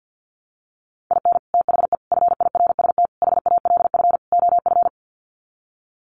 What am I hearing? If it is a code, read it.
Morse code, heard as a keyed tone, about 48 wpm, 700 Hz. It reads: IN THE FIRST HALF OF
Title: Sets of 5 Words 50wpm